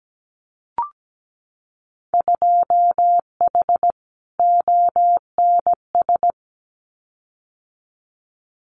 Morse code, heard as a keyed tone, about 17 wpm, 700 Hz. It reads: 2H ONS
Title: Winter Field Day Exchanges Encore 17wpm